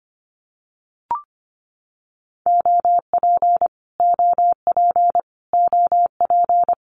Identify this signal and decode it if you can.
Morse code, heard as a keyed tone, about 25 wpm, 700 Hz. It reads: OP OP OP
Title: Abbreviations Repeated 3x - 25wpm